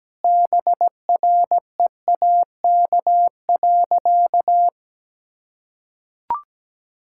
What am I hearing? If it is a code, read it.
Morse code, heard as a keyed tone, about 17 wpm, 700 Hz. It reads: BREAK.
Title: Sentences from Top 1000 Words - Review Part 6 - 17wpm